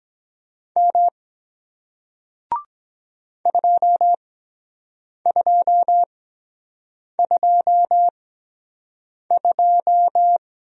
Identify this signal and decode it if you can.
Morse code, heard as a keyed tone, about 23 wpm, spaced as for 7 wpm, 700 Hz. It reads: M 2222